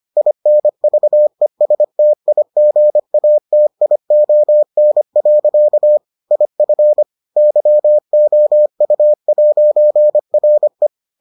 Morse code, heard as a keyed tone, about 25 wpm, 600 Hz. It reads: INVESTIGATION. IF YOU'RE